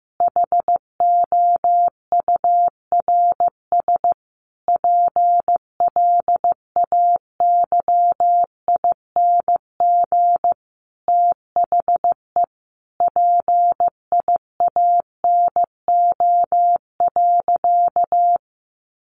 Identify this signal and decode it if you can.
Morse code, heard as a keyed tone, about 15 wpm, 700 Hz. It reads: HOURS PLAYING THE PIANO.
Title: Additional Easy Phrases 15wpm